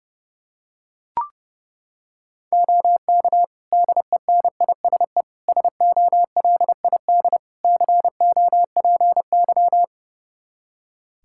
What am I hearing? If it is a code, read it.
Morse code, heard as a keyed tone, about 30 wpm, 700 Hz. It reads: OK DENISE SOLID COPY
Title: K1USN SST QSO Practice 30wpm